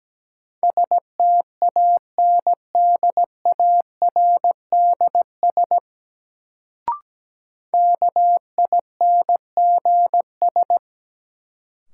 Morse code, heard as a keyed tone, about 17 wpm, 700 Hz. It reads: STANDARDS KINGS
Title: Sentences from Top 1000 Words - Review Part 6 - New Word Drill - 17wpm